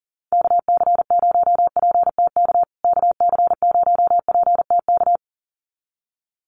Morse code, heard as a keyed tone, about 40 wpm, 700 Hz. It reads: KC0PTK KC0PTK
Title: USA Call Signs Repeated 2x 40wpm